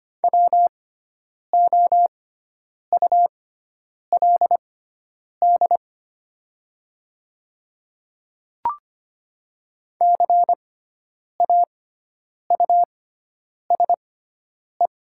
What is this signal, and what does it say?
Morse code, heard as a keyed tone, about 25 wpm, spaced as for 9 wpm, 700 Hz. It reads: WOULD CAUSE
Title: 6x Character Spacing - Top 200 Words - 25wpm